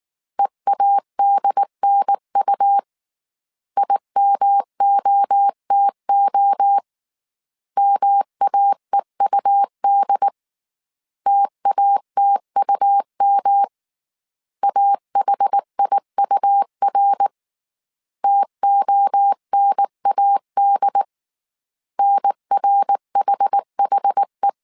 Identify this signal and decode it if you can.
Morse code, heard as a keyed tone, about 19 wpm, 800 Hz. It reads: EADNU IMOTO MAEUD TATUM AHIUR TONAD NRHHE